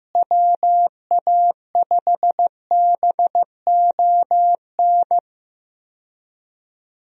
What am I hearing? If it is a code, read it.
Morse code, heard as a keyed tone, about 15 wpm, 700 Hz. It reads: WA5BON
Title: USA Call Signs - 6 Characters - Repeated 2x - 15wpm